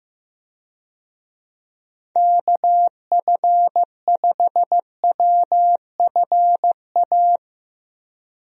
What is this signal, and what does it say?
Morse code, heard as a keyed tone, about 16 wpm, 700 Hz. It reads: KF5WFA
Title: USA Call Signs - 6 Characters - Repeated 2x - 15wpm